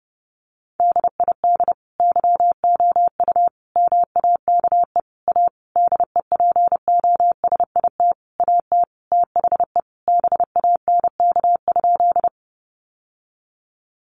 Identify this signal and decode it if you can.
Morse code, heard as a keyed tone, about 30 wpm, 700 Hz. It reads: DID YOU MAKE A DEPOSIT AT THE BANK?
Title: Sentences from Top 1300 Words 30wpm